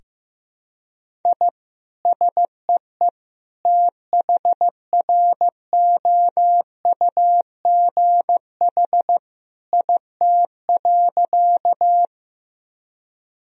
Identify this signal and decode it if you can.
Morse code, heard as a keyed tone, about 15 wpm, 700 Hz. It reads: I SEE THROUGH IT.
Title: Sentences from Top 1000 Words - Review Part 1 - 15wpm